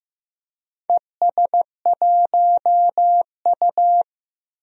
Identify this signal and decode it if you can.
Morse code, heard as a keyed tone, about 15 wpm, 700 Hz. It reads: ES1U
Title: DXCC-100 Call Signs - 4 Characters - Repeated 2x - 15wpm